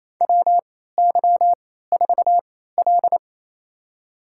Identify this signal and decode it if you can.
Morse code, heard as a keyed tone, about 28 wpm, spaced as for 16 wpm, 700 Hz. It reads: WY4L
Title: USA Call Signs - 4 Characters - 3x Character Spacing - 28wpm